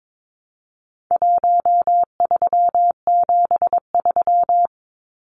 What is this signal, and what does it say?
Morse code, heard as a keyed tone, about 22 wpm, 700 Hz. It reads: 1373
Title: Numbers - Contest Style - 4-Digit 22wpm